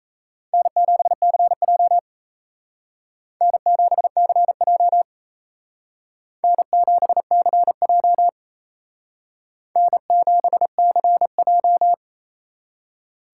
Morse code, heard as a keyed tone, about 36 wpm, 700 Hz. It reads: N7CJ N7CJ N7CJ N7CJ